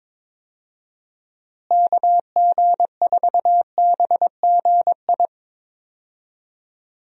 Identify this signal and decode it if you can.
Morse code, heard as a keyed tone, about 22 wpm, 700 Hz. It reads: KG4BGI
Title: USA Call Signs - 6 Characters - Repeated 2x - 22wpm